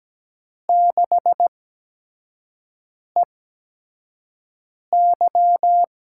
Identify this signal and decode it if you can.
Morse code, heard as a keyed tone, about 17 wpm, spaced as for 5 wpm, 700 Hz. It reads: 6EY
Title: USA Call Signs - 4 Characters - 8x Character Spacing - 17wpm